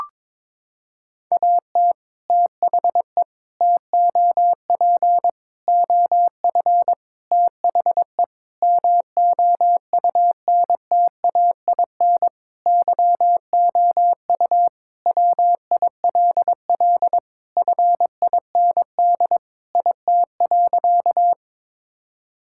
Morse code, heard as a keyed tone, about 22 wpm, 700 Hz. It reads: AT THE TOP OF THE MOUNTAIN YOU WILL FIND IT.
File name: practice / 22wpm / Sentences with 3-Word Phrases - 22wpm